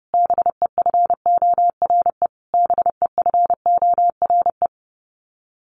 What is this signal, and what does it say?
Morse code, heard as a keyed tone, about 30 wpm, 700 Hz. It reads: BEFORE BEFORE
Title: Top 500 Words Repeated 30wpm